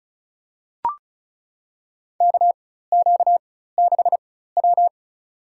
Morse code, heard as a keyed tone, about 36 wpm, spaced as for 16 wpm, 700 Hz. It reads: KQ6W
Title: USA Call Signs - 4 Characters - 4x Character Spacing - 35wpm